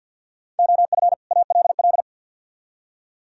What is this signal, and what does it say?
Morse code, heard as a keyed tone, about 66 wpm, spaced as for 58 wpm, 700 Hz. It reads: OF ALL
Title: Speed-Racing - Sets of 2 Words 50wpm